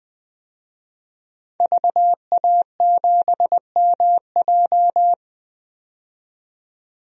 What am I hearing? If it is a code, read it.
Morse code, heard as a keyed tone, about 20 wpm, 700 Hz. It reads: VA7MJ
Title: DXCC-100 Call Signs - 5 Characters - 20wpm